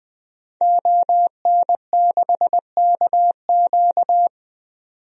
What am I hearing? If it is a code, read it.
Morse code, heard as a keyed tone, about 20 wpm, 700 Hz. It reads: ON6KQ
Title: DXCC-100 Call Signs - 5 Characters - 20wpm